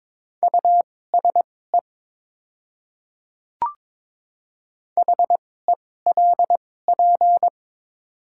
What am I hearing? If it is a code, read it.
Morse code, heard as a keyed tone, about 22 wpm, spaced as for 16 wpm, 700 Hz. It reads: USE HELP
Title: 2x Character Spacing - Top 200 Words - 22wpm